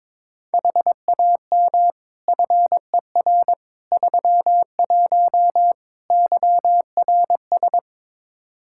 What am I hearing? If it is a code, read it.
Morse code, heard as a keyed tone, about 22 wpm, 700 Hz. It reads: HAM FER 31 YRS
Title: K1USN SST QSO Practice 22wpm